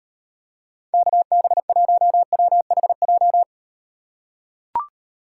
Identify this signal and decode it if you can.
Morse code, heard as a keyed tone, about 38 wpm, 700 Hz. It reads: KB1WHJ